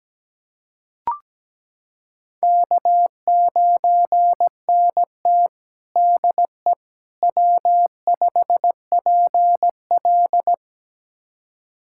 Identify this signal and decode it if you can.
Morse code, heard as a keyed tone, about 17 wpm, 700 Hz. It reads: K9NT DE W5PL
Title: USA Callsign DE Callsign - 4 Characters - 17wpm